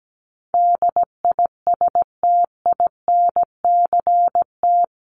Morse code, heard as a keyed tone, about 17 wpm, 700 Hz. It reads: DISTINCT
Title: Top 1501-1750 Words 17wpm